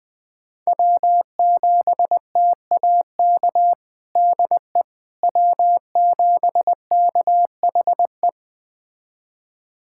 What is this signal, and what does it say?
Morse code, heard as a keyed tone, about 21 wpm, 700 Hz. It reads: W7TAK DE W7KHE
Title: USA Callsign DE Callsign - 5 Characters - 20wpm